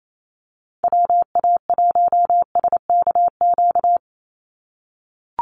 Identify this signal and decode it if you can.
Morse code, heard as a keyed tone, about 28 wpm, 700 Hz. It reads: WA1SKQ
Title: Call Signs - 28wpm